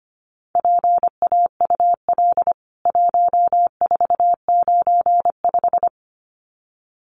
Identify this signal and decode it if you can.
Morse code, heard as a keyed tone, about 25 wpm, 700 Hz. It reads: PAUL 1495
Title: CWT Exchange 25wpm